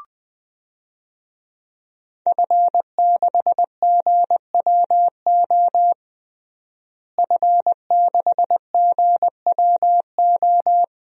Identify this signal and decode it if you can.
Morse code, heard as a keyed tone, about 21 wpm, 700 Hz. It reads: F6GWO F6GWO
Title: DXCC-100 Call Signs - 5 Characters - Repeated 3x - 20wpm